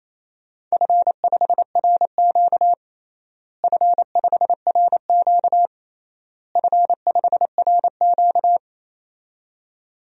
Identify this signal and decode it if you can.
Morse code, heard as a keyed tone, about 28 wpm, 700 Hz. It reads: F5RQ F5RQ F5RQ
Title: DXCC-100 Call Signs - 4 Characters - Repeated 3x - 28wpm